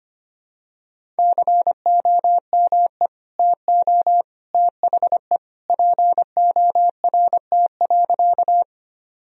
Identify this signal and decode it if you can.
Morse code, heard as a keyed tone, about 25 wpm, 700 Hz. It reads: COME TO THE PORT.
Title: Sentences from Top 1000 Words - Review Part 1 - 25wpm